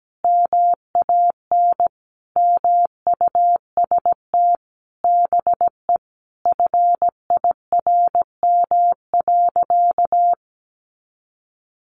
Morse code, heard as a keyed tone, about 17 wpm, 700 Hz. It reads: MAN MUST BE FIRM.
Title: Sentences from Top 1400 Words 17wpm